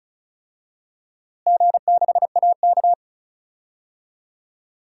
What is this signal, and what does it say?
Morse code, heard as a keyed tone, about 34 wpm, 700 Hz. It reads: G6AK
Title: DXCC-100 Call Signs - 4 Characters - Repeated 3x - 35wpm